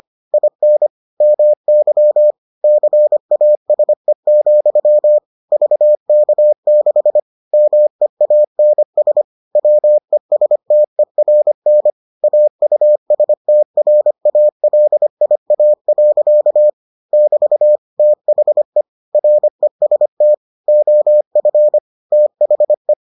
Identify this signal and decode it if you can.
Morse code, heard as a keyed tone, about 25 wpm, 600 Hz. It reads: IN MY CASE, VK6 MEANS WESTERN AUSTRALIA. <BT> THE REST OF THE